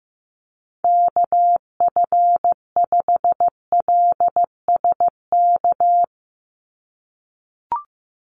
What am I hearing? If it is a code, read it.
Morse code, heard as a keyed tone, about 14 wpm, 700 Hz. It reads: KF5LSK
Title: USA Call Signs 15wpm